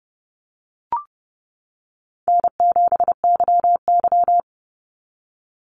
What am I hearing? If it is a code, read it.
Morse code, heard as a keyed tone, about 30 wpm, 700 Hz. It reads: N7YY
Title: NA SOTA Chaser Callsigns 30wpm